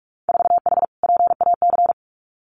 Morse code, heard as a keyed tone, about 46 wpm, 700 Hz. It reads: 4H PAC
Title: Winter Field Day Exchanges Encore 45wpm